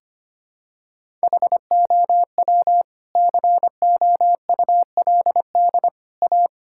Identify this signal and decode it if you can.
Morse code, heard as a keyed tone, about 25 wpm, 700 Hz. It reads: HOW COULD A
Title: Sentences from Top 1000 Words - Review Part 6 - 25wpm